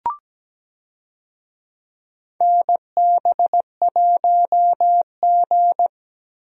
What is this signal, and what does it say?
Morse code, heard as a keyed tone, about 17 wpm, 700 Hz. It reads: NB1G
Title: USA Call Signs - 4 Characters - Repeated 3x - 17wpm